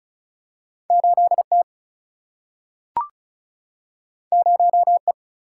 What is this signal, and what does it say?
Morse code, heard as a keyed tone, about 36 wpm, 700 Hz. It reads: 8T 0E
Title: Single-Letter Single-Number 35wpm